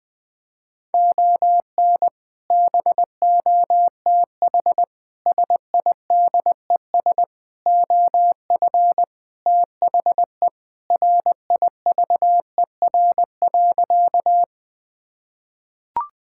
Morse code, heard as a keyed tone, about 21 wpm, 700 Hz. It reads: ON BOTH SIDES OF THE RIVER.
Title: Sentences with 4-Word Phrases - 20wpm